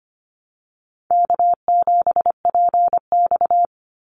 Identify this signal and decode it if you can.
Morse code, heard as a keyed tone, about 25 wpm, 700 Hz. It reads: K7PX
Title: NA SOTA Chaser Callsigns 25wpm